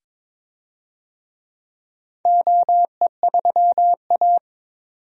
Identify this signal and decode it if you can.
Morse code, heard as a keyed tone, about 22 wpm, 700 Hz. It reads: OE3A
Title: DXCC-100 Call Signs - 4 Characters - 22wpm